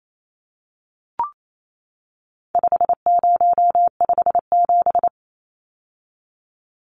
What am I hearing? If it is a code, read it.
Morse code, heard as a keyed tone, about 28 wpm, 700 Hz. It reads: 5057
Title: Numbers - 4-Digit 28wpm